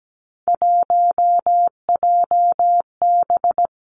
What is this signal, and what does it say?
Morse code, heard as a keyed tone, about 17 wpm, 700 Hz. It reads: 1JB
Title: Three Challenging Letter-Number Combinations 17wpm